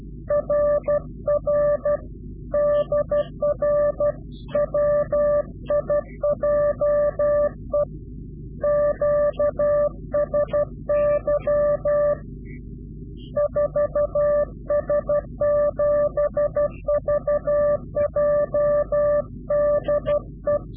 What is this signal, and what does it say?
Morse code, heard as a keyed tone, about 12 wpm, 590 Hz. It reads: RR DR WIJE QSY 4S7VJ DE
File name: CW-QSO / Directional Call / Directional QSO